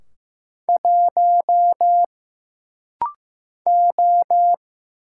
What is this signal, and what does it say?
Morse code, heard as a keyed tone, about 15 wpm, 700 Hz. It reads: 1 O
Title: Single Letter-Number Alt - Rapid-Fire 15wpm